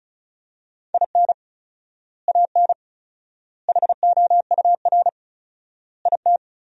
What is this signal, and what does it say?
Morse code, heard as a keyed tone, about 36 wpm, 700 Hz. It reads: IN AN HOUR IT
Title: 4x Word-Spacing - Sentences from Top 400 Words - 35wpm